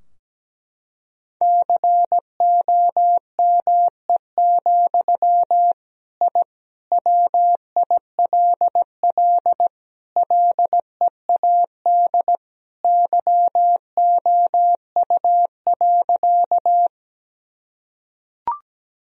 Morse code, heard as a keyed tone, about 17 wpm, 700 Hz. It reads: COME, I WILL LEAD YOU.
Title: Sentences from Top 1000 Words - Review Part 2 - 17wpm